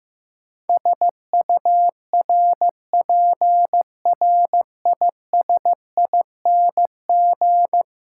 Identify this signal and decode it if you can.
Morse code, heard as a keyed tone, about 15 wpm, 700 Hz. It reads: SURPRISING
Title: Sets of 5 Words - New Word Drill - 15wpm